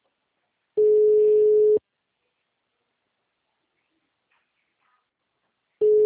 Ringback tone (Europe, 425 Hz).